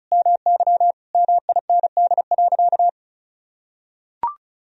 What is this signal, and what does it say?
Morse code, heard as a keyed tone, about 36 wpm, 700 Hz. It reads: MY MIND.
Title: Sentences from Top 1000 Words - Review Part 2 - 35wpm